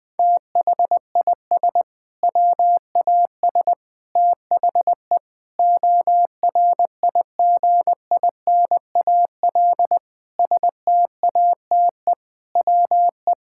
Morse code, heard as a keyed tone, about 20 wpm, 700 Hz. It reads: THIS WAS THE ORIGINAL STATE WE
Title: Sentences from Top 1000 Words - Review Part 6 - 20wpm